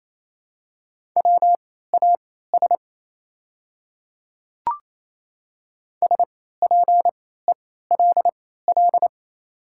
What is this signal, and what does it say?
Morse code, heard as a keyed tone, about 28 wpm, spaced as for 16 wpm, 700 Hz. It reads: WAS SPELL
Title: 3x Character Spacing - Top 200 Words - 28wpm